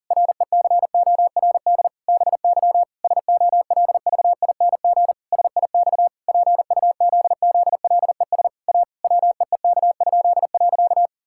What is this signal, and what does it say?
Morse code, heard as a keyed tone, about 40 wpm, 700 Hz. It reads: RECORD BY SOLVING SIX PUZZLES A WEEK?.
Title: Pangram Challenge v2 - 40wpm